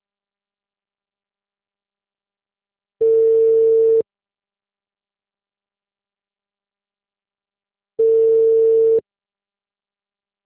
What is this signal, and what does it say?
Ringback tone (China, 450 Hz).